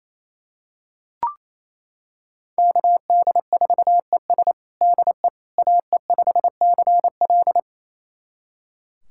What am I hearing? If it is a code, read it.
Morse code, heard as a keyed tone, about 27 wpm, 700 Hz. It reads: KD4ES DE AE5CL
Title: USA Callsign DE Callsign - 5 Characters - 28wpm